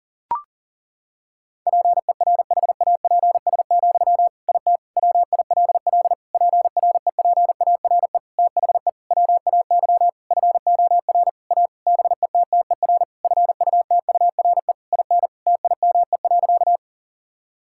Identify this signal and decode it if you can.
Morse code, heard as a keyed tone, about 40 wpm, 700 Hz. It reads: PERHAPS, IT WILL PREPARE THE WAY FOR A BETTER FUTURE IN TIME.
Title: Sentences from Top 1000 Words - Review Part 6 - 40wpm